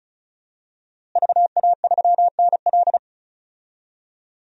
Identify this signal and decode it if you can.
Morse code, heard as a keyed tone, about 36 wpm, 700 Hz. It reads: VA3NL
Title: DXCC-100 Call Signs - 5 Characters - Repeated 3x - 35wpm